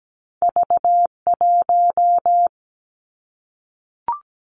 Morse code, heard as a keyed tone, about 17 wpm, 700 Hz. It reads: V 1
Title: Two Challenging Letter-Number Combinations 17wpm